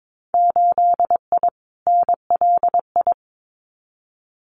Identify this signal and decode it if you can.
Morse code, heard as a keyed tone, about 22 wpm, 700 Hz. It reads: 8I NLI
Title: Winter Field Day Exchanges Encore 22wpm